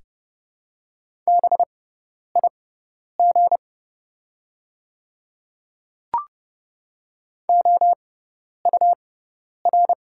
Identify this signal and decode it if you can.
Morse code, heard as a keyed tone, about 32 wpm, spaced as for 10 wpm, 700 Hz. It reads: BIG OUR
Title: 6x Character Spacing - Top 300 Words - 30wpm